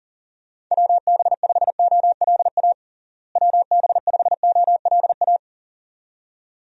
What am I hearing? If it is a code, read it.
Morse code, heard as a keyed tone, about 40 wpm, 700 Hz. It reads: WB5OLA WB5OLA
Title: USA Call Signs - 6 Characters - Repeated 2x - 40wpm